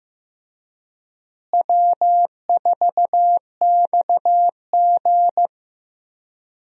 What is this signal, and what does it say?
Morse code, heard as a keyed tone, about 15 wpm, 700 Hz. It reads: W4XG
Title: USA Call Signs - 4 Characters - Repeated 2x - 15wpm